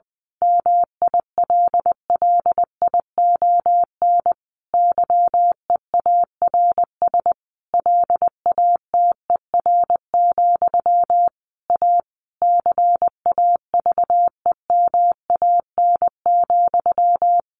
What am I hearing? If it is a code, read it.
Morse code, heard as a keyed tone, about 20 wpm, 700 Hz. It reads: MILLION YEARS LATER, A CAVEMAN,
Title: A Sound Of Thunder 20wpm